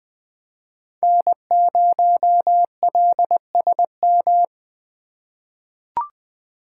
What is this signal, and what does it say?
Morse code, heard as a keyed tone, about 20 wpm, 700 Hz. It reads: N0LSM
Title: USA Call Signs - 3x Character Spacing - 20wpm